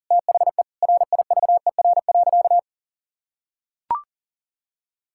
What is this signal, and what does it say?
Morse code, heard as a keyed tone, about 40 wpm, 700 Hz. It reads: THE RIVER.
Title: Sentences from Top 1600 Words 40wpm